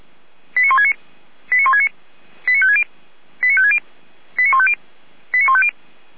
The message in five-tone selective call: 89169 89169 89560 89560 89150 89150

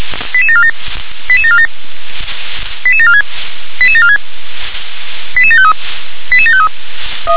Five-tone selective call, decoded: 88747 88747 88746 88746 88753 88753

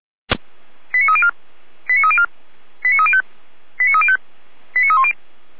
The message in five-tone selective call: 89304 89304 89305 89305 89310

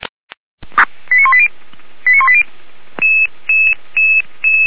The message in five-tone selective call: 89180 89180

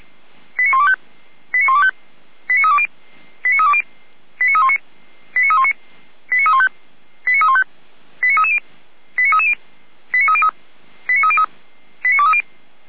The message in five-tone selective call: 89125 89125 89320 89320 89319 89319 89315 89315 89330 89330 89303 89303 89230

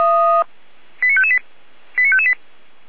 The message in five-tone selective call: 89558 89558

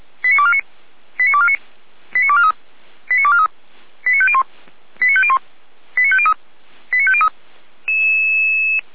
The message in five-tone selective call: 89249 89249 89243 89243 89501 89501 89503 89503

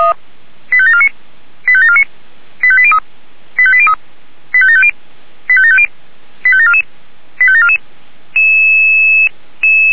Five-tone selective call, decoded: 85749 85749 85803 85803 85750 85750 85744 85744

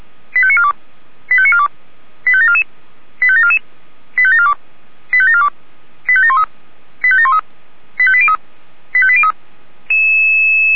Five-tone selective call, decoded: 85942 85942 85744 85744 85742 85742 85713 85713 85803 85803

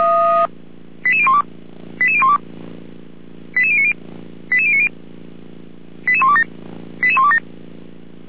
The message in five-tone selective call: 88012 88012 88090 88090 88127 88127

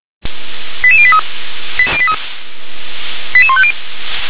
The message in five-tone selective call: 88083 88150

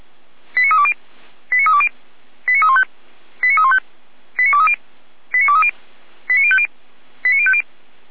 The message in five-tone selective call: 89320 89320 89315 89315 89230 89230 89050 89050

